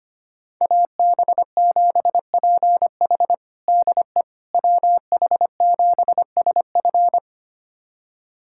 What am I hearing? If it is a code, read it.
Morse code, heard as a keyed tone, about 25 wpm, 700 Hz. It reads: AB7PH DE WH7SF
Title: USA Callsign DE Callsign - 5 Characters - 25wpm